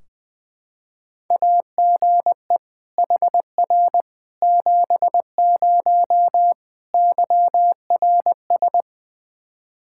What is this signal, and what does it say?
Morse code, heard as a keyed tone, about 21 wpm, 700 Hz. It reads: AGE HR 70 YRS
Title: Practice QSO - Exchange 3 - More Info - 20wpm